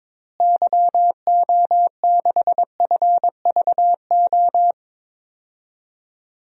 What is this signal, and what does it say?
Morse code, heard as a keyed tone, about 22 wpm, 700 Hz. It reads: YO6FVO
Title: DXCC-100 Call Signs - 6 Characters - Repeated 3x - 22wpm